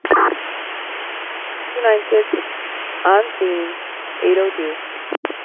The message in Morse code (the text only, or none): none